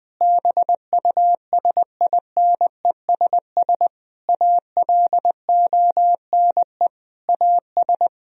Morse code, heard as a keyed tone, about 20 wpm, 700 Hz. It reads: BUSINESS ALONE AS
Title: Sentences from Top 1000 Words - Review Part 2 - 20wpm